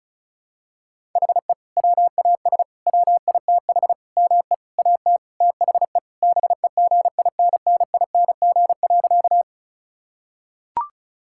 Morse code, heard as a keyed tone, about 36 wpm, 700 Hz. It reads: HE WAS WITH ME AT THE BEGINNING.
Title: Sentences from Top 1000 Words - Review Part 5 - 35wpm